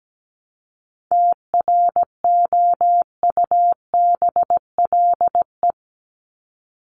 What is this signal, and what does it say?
Morse code, heard as a keyed tone, about 17 wpm, 700 Hz. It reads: TROUBLE
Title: Top 1001-1250 Words 17wpm